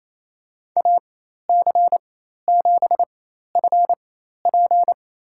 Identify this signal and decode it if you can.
Morse code, heard as a keyed tone, about 28 wpm, spaced as for 13 wpm, 700 Hz. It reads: AC7FP
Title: USA Call Signs - 5 Characters - 4x Character Spacing - 28wpm